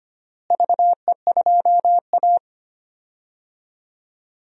Morse code, heard as a keyed tone, about 25 wpm, 700 Hz. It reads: VE2A
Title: DXCC-100 Call Signs - 4 Characters - Repeated 3x - 25wpm